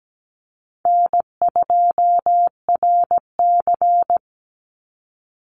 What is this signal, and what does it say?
Morse code, heard as a keyed tone, about 17 wpm, 700 Hz. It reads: N2RC
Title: Easy Call Signs Repeated 2x - 17wpm